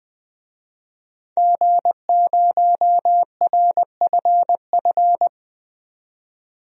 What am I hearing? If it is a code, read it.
Morse code, heard as a keyed tone, about 20 wpm, 700 Hz. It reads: G0RFF
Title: DXCC-100 Call Signs - 5 Characters - 20wpm